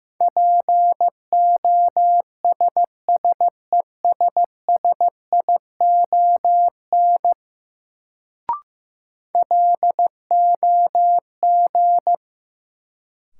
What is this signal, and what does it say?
Morse code, heard as a keyed tone, about 15 wpm, 700 Hz. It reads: POSSESSION LOG
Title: Sentences from Top 1400 Words - New Word Drill - 15wpm